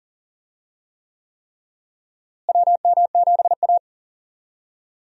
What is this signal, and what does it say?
Morse code, heard as a keyed tone, about 40 wpm, 700 Hz. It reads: WM7A
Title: USA Call Signs - 4 Characters - Repeated 3x - 40wpm